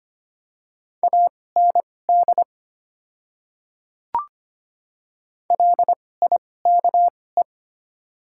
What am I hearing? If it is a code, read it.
Morse code, heard as a keyed tone, about 25 wpm, spaced as for 18 wpm, 700 Hz. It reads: AND LIKE
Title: 2x Character Spacing - Top 100 Words - 25wpm